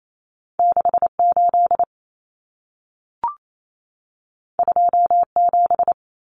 Morse code, heard as a keyed tone, about 28 wpm, 700 Hz. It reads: 68 27
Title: Numbers - 2-Digit 28wpm